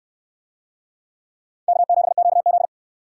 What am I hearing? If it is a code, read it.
Morse code, heard as a keyed tone, about 64 wpm, 700 Hz. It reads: K7QZ